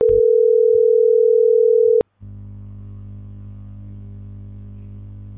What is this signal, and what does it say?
Ringback tone (North America), 440 + 480 Hz.